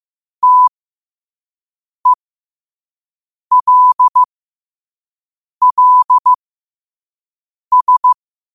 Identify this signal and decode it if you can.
Morse code, heard as a keyed tone, about 15 wpm, spaced as for 5 wpm, 1000 Hz. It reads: TELLS
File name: 5wpm